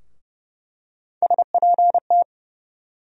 Morse code, heard as a keyed tone, about 30 wpm, 700 Hz. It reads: SPT
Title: USA Call Sign Postfixes - 3 Letters 30wpm